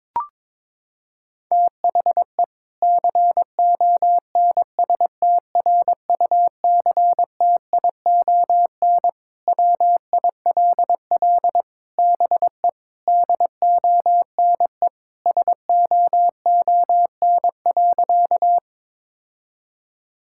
Morse code, heard as a keyed tone, about 22 wpm, 700 Hz. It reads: THE CONSTRUCTION WILL BE DONE SOON.
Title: Sentences from Top 1600 Words 22wpm